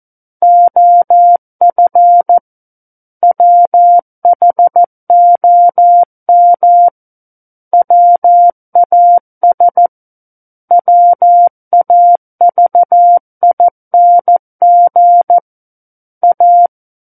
Morse code, heard as a keyed tone, about 14 wpm, 700 Hz. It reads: OF WHOM WAS WAVING A
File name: Book One / 05-Chapter 5 - 14 WPM